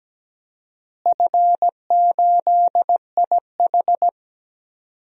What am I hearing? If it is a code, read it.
Morse code, heard as a keyed tone, about 17 wpm, 700 Hz. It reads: F8IH
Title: DXCC-100 Call Signs - 4 Characters - Repeated 2x - 17wpm